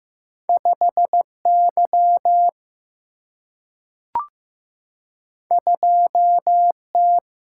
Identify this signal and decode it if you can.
Morse code, heard as a keyed tone, about 15 wpm, 700 Hz. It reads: 5Y 2T
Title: Single-Letter Single-Number 15wpm